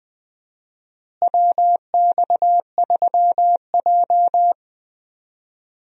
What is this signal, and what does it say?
Morse code, heard as a keyed tone, about 20 wpm, 700 Hz. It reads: WX3J
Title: USA Call Signs - 4 Characters - Repeated 2x - 20wpm